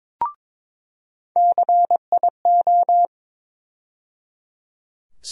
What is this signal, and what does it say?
Morse code, heard as a keyed tone, about 21 wpm, 700 Hz. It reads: CIO
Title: USA Call Sign Postfixes - 3 Letters 22wpm